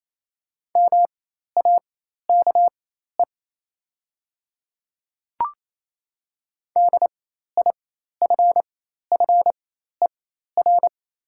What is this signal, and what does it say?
Morse code, heard as a keyed tone, about 28 wpm, spaced as for 13 wpm, 700 Hz. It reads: MAKE DIFFER
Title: 4x Character Spacing - Top 300 Words - 28wpm